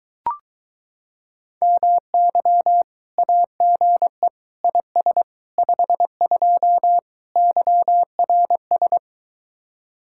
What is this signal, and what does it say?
Morse code, heard as a keyed tone, about 24 wpm, 700 Hz. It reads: MY AGE IS 52 YRS